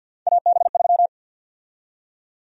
Morse code, heard as a keyed tone, about 53 wpm, 700 Hz. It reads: AB3
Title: USA Call Sign Prefixes - 50wpm